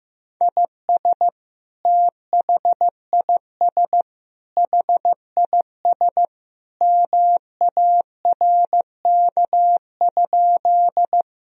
Morse code, heard as a keyed tone, about 15 wpm, 700 Hz. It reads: IS THIS HIS MARK?
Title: Sentences from Top 1000 Words - Review Part 1 - 15wpm